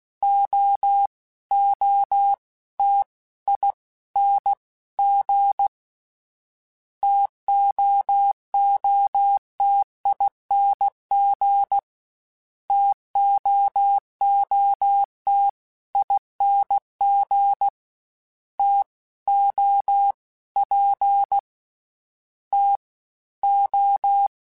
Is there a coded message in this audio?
Yes: Morse code from 0.2 to 24.3 s, heard as a keyed tone, about 16 wpm, 800 Hz. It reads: O O T I N G TOOTING TOOT ING T O P T O